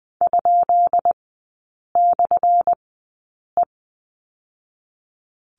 Morse code, heard as a keyed tone, about 20 wpm, spaced as for 8 wpm, 700 Hz. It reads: ?/E
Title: 218 Lesson 35e - Number 0 - Three Characters 20wpm